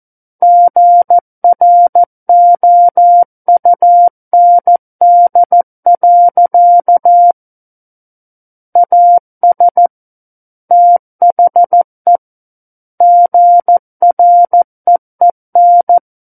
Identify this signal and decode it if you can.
Morse code, heard as a keyed tone, about 14 wpm, 700 Hz. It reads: GROUND. AS THE GREEN
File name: Book One / 05-Chapter 5 - 14 WPM